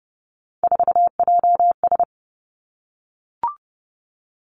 Morse code, heard as a keyed tone, about 30 wpm, 700 Hz. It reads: 4JS
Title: Three Challenging Letter-Number Combinations 30wpm